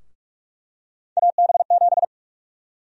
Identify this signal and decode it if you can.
Morse code, heard as a keyed tone, about 46 wpm, 700 Hz. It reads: AB7
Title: USA Call Sign Prefixes - Repeated 2x - 45wpm